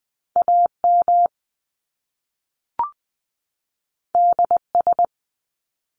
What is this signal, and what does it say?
Morse code, heard as a keyed tone, about 20 wpm, 700 Hz. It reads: AM DS
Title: Top Two Letter Combinations in Words 20wpm